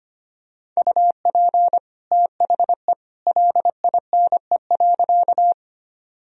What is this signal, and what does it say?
Morse code, heard as a keyed tone, about 25 wpm, 700 Hz. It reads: UP THE LINE.
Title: Sentences from Top 1000 Words - Review Part 1 - 25wpm